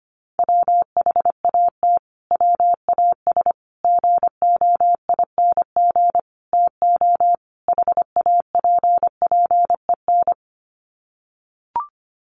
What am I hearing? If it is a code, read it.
Morse code, heard as a keyed tone, about 25 wpm, 700 Hz. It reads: WHAT WAS GOING TO HAPPEN
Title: Sets of 5 Words Encore 25wpm